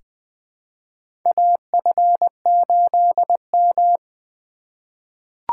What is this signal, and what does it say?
Morse code, heard as a keyed tone, about 20 wpm, 700 Hz. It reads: AF8M
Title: USA Call Signs - 4 Characters - 4x Character Spacing - 20wpm